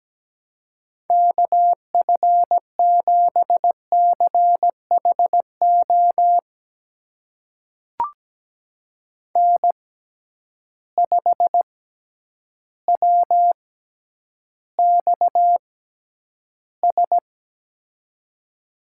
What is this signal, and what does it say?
Morse code, heard as a keyed tone, about 17 wpm, spaced as for 9 wpm, 700 Hz. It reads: KF7CHO N5WXS
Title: USA Call Signs - 6x Character Spacing - 17wpm